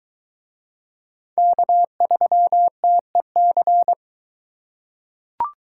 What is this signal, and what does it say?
Morse code, heard as a keyed tone, about 22 wpm, 700 Hz. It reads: K3TEC